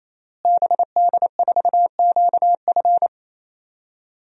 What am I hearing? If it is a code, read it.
Morse code, heard as a keyed tone, about 28 wpm, 700 Hz. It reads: BD4QF
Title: DXCC-100 Call Signs - 5 Characters - Repeated 3x - 28wpm